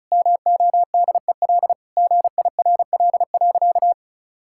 Morse code, heard as a keyed tone, about 34 wpm, 700 Hz. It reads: MODEL GIRL.
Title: Sentences from Top 1000 Words - Review Part 5 - 35wpm